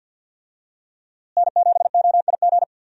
Morse code, heard as a keyed tone, about 52 wpm, 700 Hz. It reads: N7OIG
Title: USA Call Signs - 5 Characters - Repeated 3x - 50wpm